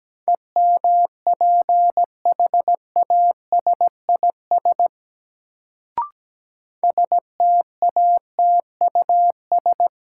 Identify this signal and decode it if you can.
Morse code, heard as a keyed tone, about 17 wpm, 700 Hz. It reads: EMPHASIS STATUS
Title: Sentences from Top 1400 Words - New Word Drill - 17wpm